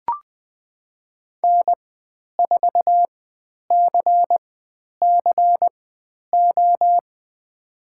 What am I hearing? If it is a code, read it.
Morse code, heard as a keyed tone, about 20 wpm, spaced as for 10 wpm, 700 Hz. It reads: N4CCO
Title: 095 Lesson 16g - Letter c - Call Signs 20wpm